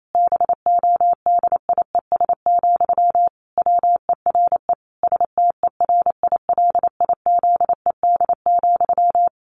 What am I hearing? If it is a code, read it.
Morse code, heard as a keyed tone, about 28 wpm, 700 Hz. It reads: BODIES, WERE STERILIZED,
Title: A Sound Of Thunder 28wpm